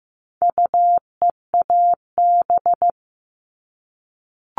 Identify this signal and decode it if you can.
Morse code, heard as a keyed tone, about 15 wpm, 700 Hz. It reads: UEAB
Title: All Cut Numbers - 4-Digit 15wpm